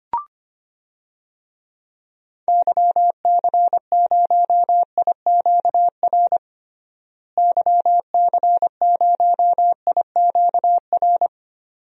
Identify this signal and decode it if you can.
Morse code, heard as a keyed tone, about 25 wpm, 700 Hz. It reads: YC0IQR YC0IQR
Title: DXCC-100 Call Signs - 6 Characters - Repeated 2x - 25wpm